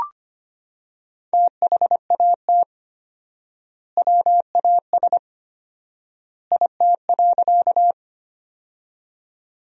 Morse code, heard as a keyed tone, about 25 wpm, 700 Hz. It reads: THAT WAS IT.
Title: 4x Word-Spacing - Sentences from Top 100 Words - 25wpm